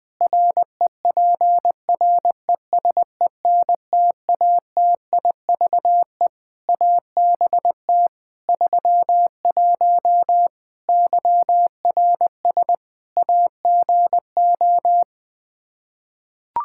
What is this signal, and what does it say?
Morse code, heard as a keyed tone, about 21 wpm, 700 Hz. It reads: REPRESENTATIVE ABT 31 YRS AGO
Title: Practice QSO - Exchange 3 - More Info - 20wpm